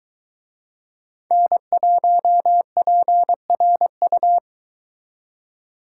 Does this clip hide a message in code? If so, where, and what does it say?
Yes: Morse code from 1.3 to 4.4 s, heard as a keyed tone, about 22 wpm, 700 Hz. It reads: N1PRU